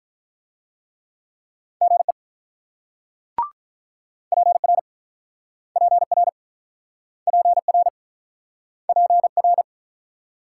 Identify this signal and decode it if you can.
Morse code, heard as a keyed tone, about 48 wpm, 690 Hz. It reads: GE PR PR PR PR